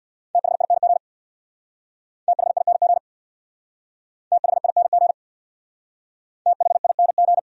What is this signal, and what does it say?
Morse code, heard as a keyed tone, about 66 wpm, 690 Hz. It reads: THING THING THING THING